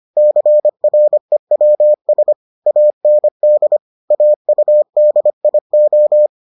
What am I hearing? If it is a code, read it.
Morse code, heard as a keyed tone, about 25 wpm, 600 Hz. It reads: CREWS AND AUDIO